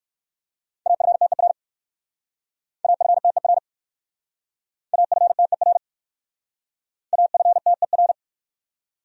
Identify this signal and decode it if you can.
Morse code, heard as a keyed tone, about 60 wpm, 700 Hz. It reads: AFTER AFTER AFTER AFTER